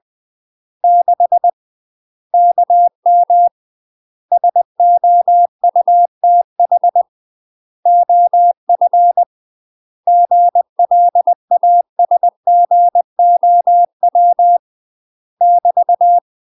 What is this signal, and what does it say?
Morse code, heard as a keyed tone, about 20 wpm, 700 Hz. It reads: 6 KM SOUTH OF GLASGOW <BT>
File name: UK MASTER QSOs TRAINING / 20WPM / UK MASTER QSOs 0063